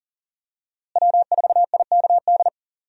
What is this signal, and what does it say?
Morse code, heard as a keyed tone, about 37 wpm, 700 Hz. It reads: W4IKD
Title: USA Call Signs - 5 Characters - Repeated 2x - 40wpm